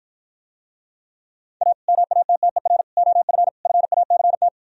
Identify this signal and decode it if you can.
Morse code, heard as a keyed tone, about 52 wpm, 700 Hz. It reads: A MATTER OF FACT